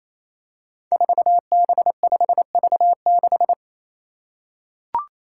Morse code, heard as a keyed tone, about 27 wpm, 700 Hz. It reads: 4B5V6
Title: Five Challenging Letter-Number Combinations 28wpm